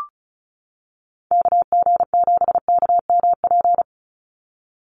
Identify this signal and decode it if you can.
Morse code, heard as a keyed tone, about 34 wpm, 700 Hz. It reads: KG7KMP
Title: USA Call Signs 35wpm